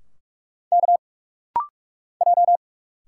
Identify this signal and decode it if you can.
Morse code, heard as a keyed tone, about 46 wpm, 700 Hz. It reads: K J
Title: Single Letters - Rapid-Fire 45wpm